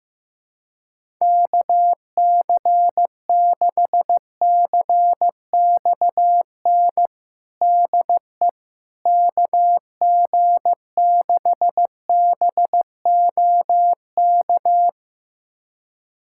Morse code, heard as a keyed tone, about 15 wpm, 700 Hz. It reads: KC6CXN DE KG6BOK
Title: USA Callsign DE Callsign - 6 Characters - 15wpm